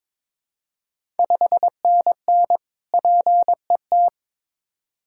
Morse code, heard as a keyed tone, about 23 wpm, 700 Hz. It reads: 5NN PET
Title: Ontario QSO Party Exchange 22wpm